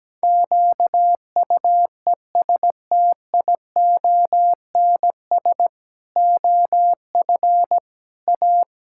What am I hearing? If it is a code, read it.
Morse code, heard as a keyed tone, about 17 wpm, 700 Hz. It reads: QUESTIONS OF A
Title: Pangram Challenge v2 - 17wpm